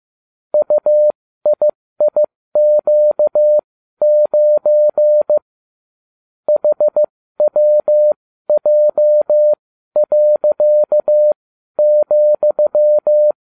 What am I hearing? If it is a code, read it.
Morse code, heard as a keyed tone, about 15 wpm, spaced as for 12 wpm, 600 Hz. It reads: UIIQ9 HWJ.,